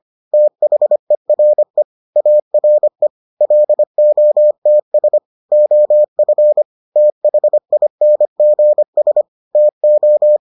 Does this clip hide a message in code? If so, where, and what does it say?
Yes: Morse code from 0.3 to 10.4 s, heard as a keyed tone, about 25 wpm, 600 Hz. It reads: THERE ARE LOTS OF THINGS TO